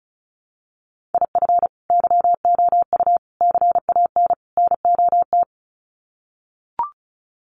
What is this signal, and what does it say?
Morse code, heard as a keyed tone, about 34 wpm, 700 Hz. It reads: IF YOU CAN NOT
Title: Sets of 4 Words 35wpm